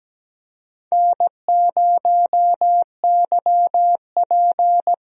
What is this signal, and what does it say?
Morse code, heard as a keyed tone, about 17 wpm, 700 Hz. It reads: N0YP
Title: USA Call Signs - 4 Characters - 17wpm